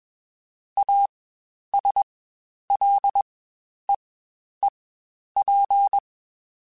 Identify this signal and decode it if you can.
Morse code, heard as a keyed tone, about 21 wpm, spaced as for 10 wpm, 800 Hz. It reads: ASLEEP
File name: Quotes Spaced / Quotes Spaced-mp3s / ChildAsleep sp @20